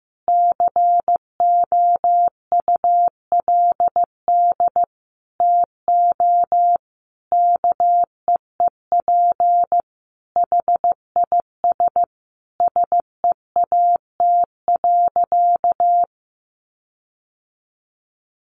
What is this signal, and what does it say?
Morse code, heard as a keyed tone, about 15 wpm, 700 Hz. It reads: COULD TO KEEP HIS SEAT.
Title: Sentences from Top 1500 Words 15wpm